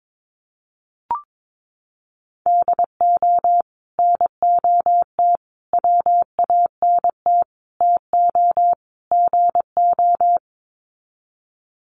Morse code, heard as a keyed tone, about 22 wpm, 700 Hz. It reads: DO NOT WANT TO GO
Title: Sets of 5 Words 22wpm